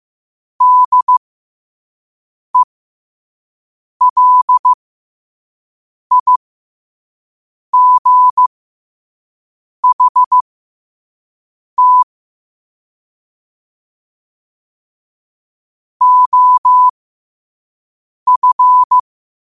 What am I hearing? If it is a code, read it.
Morse code, heard as a keyed tone, about 15 wpm, spaced as for 5 wpm, 1000 Hz. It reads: DELIGHT OF
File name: qotdfiles / 5wpm